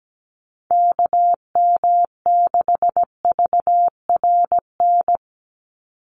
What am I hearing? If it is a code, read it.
Morse code, heard as a keyed tone, about 17 wpm, 700 Hz. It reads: KM6VRN
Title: USA Call Signs 17wpm